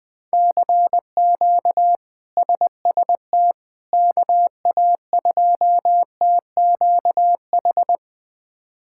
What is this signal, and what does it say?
Morse code, heard as a keyed tone, about 20 wpm, 700 Hz. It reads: CQ SST KA2TQH
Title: K1USN SST QSO Practice - 20wpm